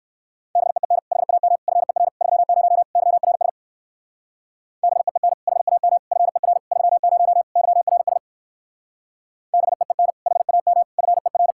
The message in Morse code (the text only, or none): BEEN HAM FER 30 YRS BEEN HAM FER 30 YRS BEEN HAM FER